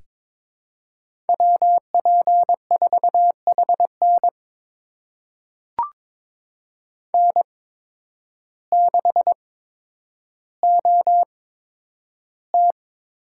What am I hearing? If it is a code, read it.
Morse code, heard as a keyed tone, about 22 wpm, spaced as for 10 wpm, 700 Hz. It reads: WP4HN N6OT
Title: USA Call Signs - 5 Characters - 8x Character Spacing - 22wpm